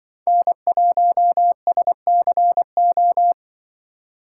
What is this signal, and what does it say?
Morse code, heard as a keyed tone, about 24 wpm, 700 Hz. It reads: N1SCO
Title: Speed Racing - USA 1x3 Call Signs 20wpm